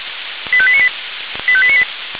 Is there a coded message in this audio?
Yes: five-tone selective call (ZVEI) from 0.5 to 1.8 s, heard as quick runs of five tones, tones of about 70 ms. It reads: 85898 85898